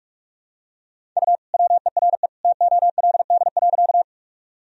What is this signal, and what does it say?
Morse code, heard as a keyed tone, about 43 wpm, 700 Hz. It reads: U WERE TOLD.
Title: Sentences from Top 400 Words and Abbreviations 45wpm